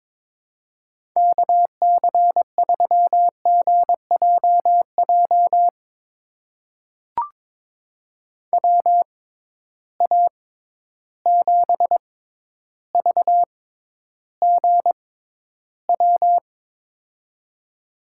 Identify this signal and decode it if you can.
Morse code, heard as a keyed tone, about 22 wpm, spaced as for 11 wpm, 700 Hz. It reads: KC3GJJ WA7VGW
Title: USA Call Signs - 6 Characters - 6x Character Spacing - 22wpm